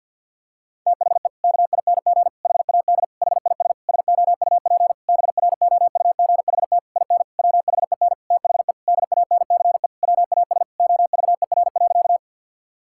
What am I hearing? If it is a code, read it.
Morse code, heard as a keyed tone, about 50 wpm, 700 Hz. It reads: THE KING HAD HIS SOUP BROUGHT IN WHEN THE DANCE WAS OVER.
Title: Sentences from Top 1000 Words - Review Part 6 - 50wpm